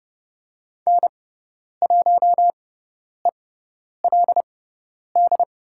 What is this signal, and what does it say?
Morse code, heard as a keyed tone, about 30 wpm, spaced as for 10 wpm, 700 Hz. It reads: N1ELD
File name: Lesson 16g - Letter c - Call Signs 30wpm